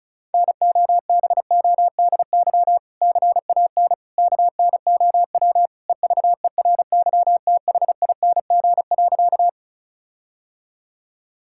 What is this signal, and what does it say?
Morse code, heard as a keyed tone, about 36 wpm, 700 Hz. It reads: NOBODY CAN KNOW EVERYTHING.
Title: Sentences from Top 1600 Words 35wpm